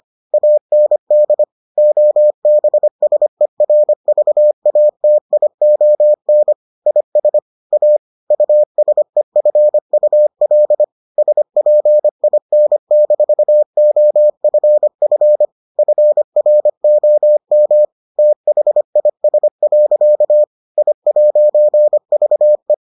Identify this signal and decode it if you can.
Morse code, heard as a keyed tone, about 25 wpm, 600 Hz. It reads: AND OBSERVATION IS A USEFUL SPIN-OFF FROM THIS. I'VE